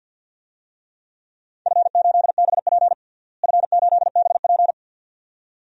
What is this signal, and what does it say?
Morse code, heard as a keyed tone, about 52 wpm, 700 Hz. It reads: F8BP F8BP
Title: DXCC-100 Call Signs - 4 Characters - Repeated 2x - 50wpm